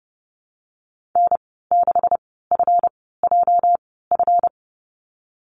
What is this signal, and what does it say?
Morse code, heard as a keyed tone, about 30 wpm, spaced as for 17 wpm, 700 Hz. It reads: N6FJF
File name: Lesson 37g - Letter x - Call Signs 30wpm